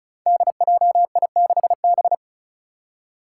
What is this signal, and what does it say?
Morse code, heard as a keyed tone, about 36 wpm, 700 Hz. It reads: DJI6B
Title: Five Challenging Letter-Number Combinations 35wpm